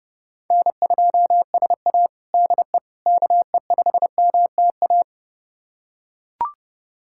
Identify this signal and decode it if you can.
Morse code, heard as a keyed tone, about 32 wpm, 700 Hz. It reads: N2SA DE KE5MTA
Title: USA Callsign DE Callsign 30wpm